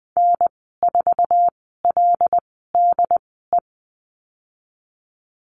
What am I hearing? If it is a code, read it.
Morse code, heard as a keyed tone, about 19 wpm, spaced as for 14 wpm, 700 Hz. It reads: N4LDE
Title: 076 Lesson 13g - Letter L - Call Signs 20wpm